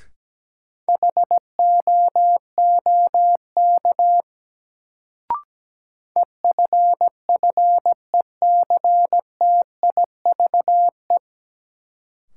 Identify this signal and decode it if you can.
Morse code, heard as a keyed tone, about 17 wpm, 700 Hz. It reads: HOOK EFFECTIVE
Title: Sentences from Top 1600 Words - New Word Drill Alt - 17wpm